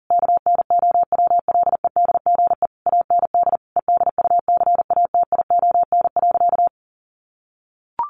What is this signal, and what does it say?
Morse code, heard as a keyed tone, about 40 wpm, 700 Hz. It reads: KNOWLEDGE AND EDUCATION.
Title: Sentences from Top 1500 Words 40wpm